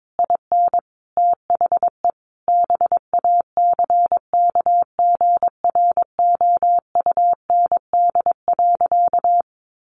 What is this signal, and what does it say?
Morse code, heard as a keyed tone, about 22 wpm, 700 Hz. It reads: IN THE BACKGROUND.
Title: Sentences from Top 1300 Words 22wpm